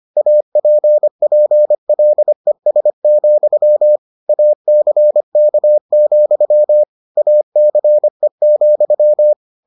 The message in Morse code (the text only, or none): APPLES, ACK, ACE,